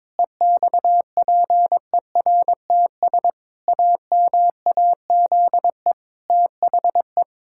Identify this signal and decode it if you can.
Morse code, heard as a keyed tone, about 22 wpm, 700 Hz. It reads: EXPERTS AMAZE THE
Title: Pangram Challenge v2 - 22wpm